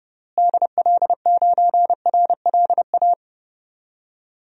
Morse code, heard as a keyed tone, about 32 wpm, 700 Hz. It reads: DL9RLA
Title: DXCC-100 Call Signs - 6 Characters - Repeated 3x - 30wpm